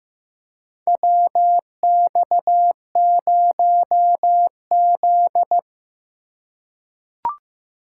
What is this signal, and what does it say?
Morse code, heard as a keyed tone, about 15 wpm, 700 Hz. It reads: WX0Z
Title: K1USN SST QSO Practice - 15wpm